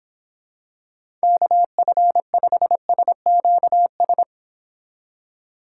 Morse code, heard as a keyed tone, about 25 wpm, 700 Hz. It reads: KF5SQS
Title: Speed Racing - USA 2x3 Call Signs 22wpm